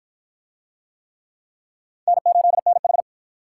Morse code, heard as a keyed tone, about 52 wpm, 700 Hz. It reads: N8NH